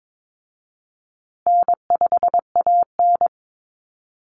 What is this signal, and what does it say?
Morse code, heard as a keyed tone, about 22 wpm, 690 Hz. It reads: N5AN
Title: Call Signs Repeated 3x - 22wpm